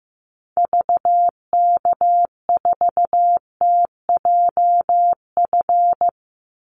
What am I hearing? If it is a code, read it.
Morse code, heard as a keyed tone, about 15 wpm, 700 Hz. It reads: VK4TJF
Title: Call Signs Repeated 3x - 15wpm